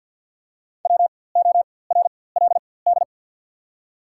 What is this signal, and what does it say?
Morse code, heard as a keyed tone, about 54 wpm, spaced as for 24 wpm, 700 Hz. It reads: WORLD
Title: 4x Character Spacing - Top 500 Words - 50wpm